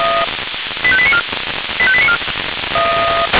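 Five-tone selective call (ZVEI), heard as quick runs of five tones, tones of about 70 ms. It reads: 85894 85894
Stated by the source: SDR-received ZVEI five-tone selective call.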